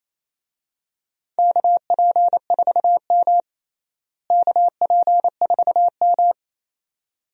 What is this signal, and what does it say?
Morse code, heard as a keyed tone, about 28 wpm, 700 Hz. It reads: KP4M KP4M
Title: DXCC-100 Call Signs - 4 Characters - Repeated 2x - 28wpm